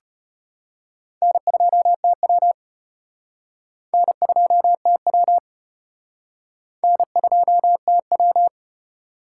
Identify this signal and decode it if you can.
Morse code, heard as a keyed tone, about 33 wpm, 700 Hz. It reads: N2TW N2TW N2TW